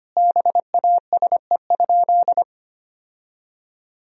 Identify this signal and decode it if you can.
Morse code, heard as a keyed tone, about 25 wpm, 700 Hz. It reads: BASE?
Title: Sentences from Top 1000 Words - Review Part 1 - 25wpm